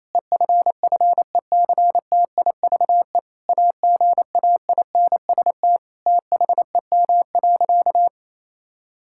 Morse code, heard as a keyed tone, about 28 wpm, 700 Hz. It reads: EFFECTIVE AGAINST THEM.
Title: Sentences from Top 1000 Words - Review Part 6 - 28wpm